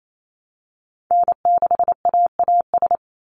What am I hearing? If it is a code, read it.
Morse code, heard as a keyed tone, about 28 wpm, 700 Hz. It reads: N6AAS
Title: USA Call Signs 28wpm